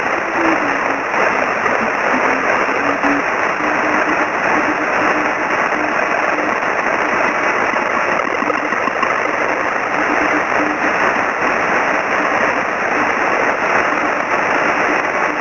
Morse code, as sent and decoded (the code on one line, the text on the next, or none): none